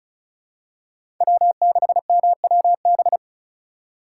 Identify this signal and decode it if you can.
Morse code, heard as a keyed tone, about 34 wpm, 700 Hz. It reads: W6MWB
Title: USA Call Signs - 5 Characters - 3x Character Spacing - 35wpm